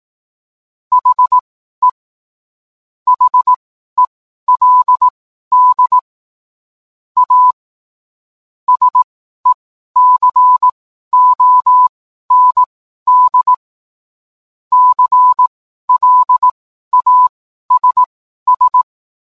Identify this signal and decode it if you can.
Morse code, heard as a keyed tone, about 18 wpm, spaced as for 12 wpm, 1000 Hz. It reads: HE HELD A SECOND CLASS